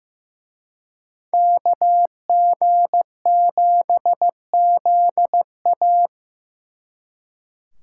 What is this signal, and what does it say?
Morse code, heard as a keyed tone, about 18 wpm, 700 Hz. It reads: KG7ZA
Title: Speed Racing - USA Call Signs 15wpm